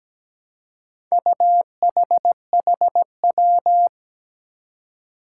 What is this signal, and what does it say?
Morse code, heard as a keyed tone, about 17 wpm, 700 Hz. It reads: UHHW
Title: Four Challenging Letter-Number Combinations 17wpm